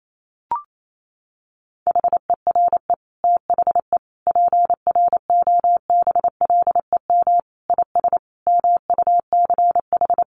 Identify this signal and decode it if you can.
Morse code, heard as a keyed tone, about 28 wpm, 700 Hz. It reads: HERE THE PROBLEM IS MUCH
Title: Sentences from Top 1200 Words 28wpm